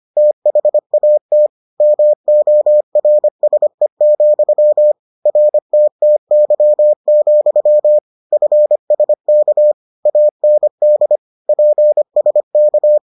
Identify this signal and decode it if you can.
Morse code, heard as a keyed tone, about 25 wpm, 600 Hz. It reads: THAT MORSE, RTTY, FSK AND PSK